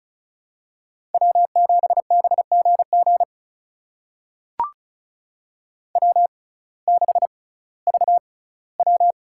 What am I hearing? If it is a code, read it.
Morse code, heard as a keyed tone, about 36 wpm, spaced as for 19 wpm, 700 Hz. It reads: W7BGG W6VW
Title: USA Call Signs - 6x Character Spacing - 35wpm